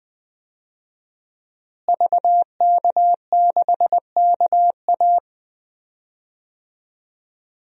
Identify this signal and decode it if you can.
Morse code, heard as a keyed tone, about 20 wpm, 700 Hz. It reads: VK6KA
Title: DXCC-100 Call Signs - 5 Characters - Repeated 2x - 20wpm